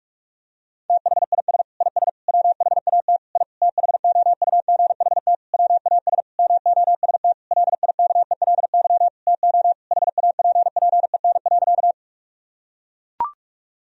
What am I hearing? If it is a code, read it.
Morse code, heard as a keyed tone, about 42 wpm, 700 Hz. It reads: THIS IS WHAT I THOUGHT WAS MOST LIKELY TO HAPPEN.
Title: Sentences from Top 1000 Words - Review Part 4 - 45wpm